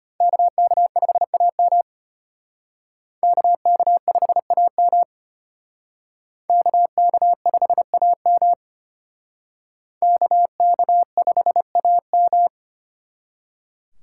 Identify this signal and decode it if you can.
Morse code, heard as a keyed tone, about 32 wpm, 700 Hz. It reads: KK5AM KK5AM KK5AM KK5AM